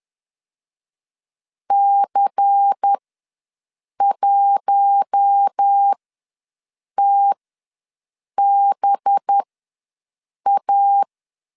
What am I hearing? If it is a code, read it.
Morse code, heard as a keyed tone, about 10 wpm, spaced as for 6 wpm, 800 Hz. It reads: C1TBA
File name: LEZIONI / 6 PKX0 / 6 P ws 10wpm 3751